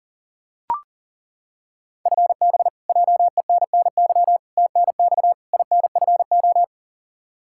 Morse code, heard as a keyed tone, about 40 wpm, 700 Hz. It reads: FB JENNY TNX INFO
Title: Practice QSO - Exchange 3 - More Info - 40wpm